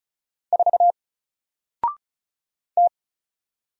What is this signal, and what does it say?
Morse code, heard as a keyed tone, about 34 wpm, 700 Hz. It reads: 4 T
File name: Single-Letter-Number-35wpm